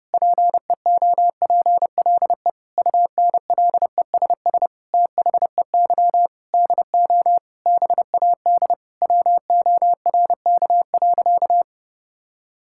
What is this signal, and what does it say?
Morse code, heard as a keyed tone, about 32 wpm, 700 Hz. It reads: PEOPLE UNLESS THEY DO BAD WORK.
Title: Sentences from Top 1000 Words - Review Part 6 - 30wpm